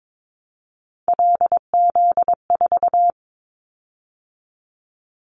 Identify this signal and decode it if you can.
Morse code, heard as a keyed tone, about 22 wpm, 700 Hz. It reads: LZ4
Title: Callsign Prefixes for Top 50 DXCC - 22wpm